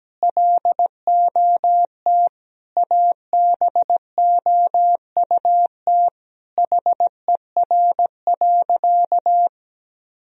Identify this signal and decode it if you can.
Morse code, heard as a keyed tone, about 17 wpm, 700 Hz. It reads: LOT ABOUT HER.
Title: Sentences from Top 1000 Words - Review Part 3 - 17wpm